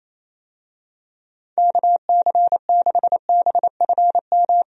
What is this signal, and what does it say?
Morse code, heard as a keyed tone, about 28 wpm, 700 Hz. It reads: KC6BFM
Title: USA Call Signs - 6 Characters - 28wpm